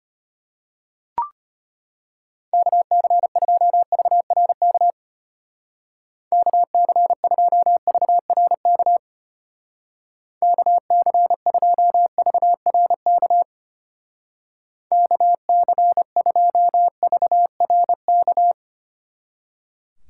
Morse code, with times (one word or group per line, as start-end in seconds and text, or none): 2.5-4.9 KC2VRK
6.3-9.0 KC2VRK
10.4-13.4 KC2VRK
14.9-18.5 KC2VRK